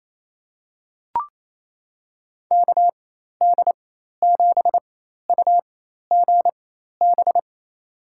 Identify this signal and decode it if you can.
Morse code, heard as a keyed tone, about 28 wpm, spaced as for 13 wpm, 700 Hz. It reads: KD7UGB
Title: USA Call Signs - 6 Characters - 4x Character Spacing - 28wpm